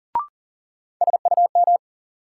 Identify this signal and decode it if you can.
Morse code, heard as a keyed tone, about 40 wpm, 700 Hz. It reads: SUM
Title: Sentences from Top 1000 Words - Review Part 6 - New Word Drill - 40wpm